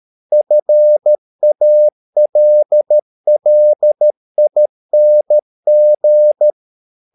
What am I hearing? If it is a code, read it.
Morse code, heard as a keyed tone, about 13 wpm, 600 Hz. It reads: FALLING